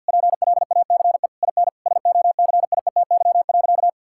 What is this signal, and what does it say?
Morse code, heard as a keyed tone, about 50 wpm, 700 Hz. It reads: PLACE IN SOCIETY.
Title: Sentences from Top 1600 Words 50wpm